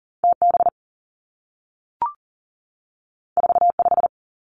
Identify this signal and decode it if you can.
Morse code, heard as a keyed tone, about 40 wpm, 700 Hz. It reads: T B 4 5
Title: Two Challenging Letter-Number Combinations 40wpm